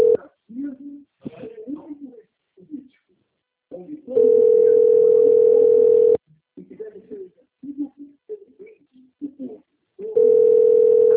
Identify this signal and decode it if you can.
Ringback tone (North America), 440 + 480 Hz, with speech.
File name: out- - - - - .49754